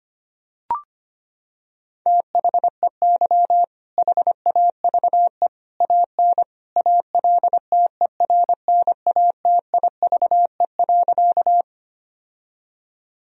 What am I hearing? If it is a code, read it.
Morse code, heard as a keyed tone, about 25 wpm, 700 Hz. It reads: THEY HAVE AN ALTERNATIVE.
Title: Sentences from Top 1000 Words - Review Part 4 - 25wpm